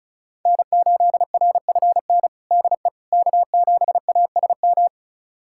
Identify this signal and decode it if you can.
Morse code, heard as a keyed tone, about 36 wpm, 700 Hz. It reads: N8RFN DE K7ASM
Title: USA Callsign DE Callsign - 5 Characters - 35wpm